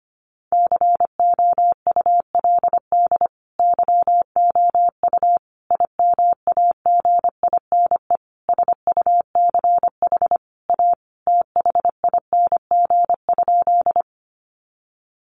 Morse code, heard as a keyed tone, about 25 wpm, 700 Hz. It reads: COULD YOU IMAGINE SUCH A THING?
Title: Sentences from Top 1400 Words 25wpm